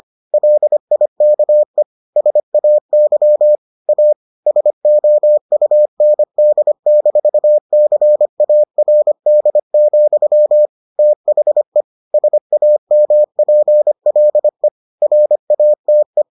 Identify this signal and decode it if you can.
Morse code, heard as a keyed tone, about 25 wpm, 600 Hz. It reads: LIKE SAY A SOUND-CARD, THE SAMPLE RATE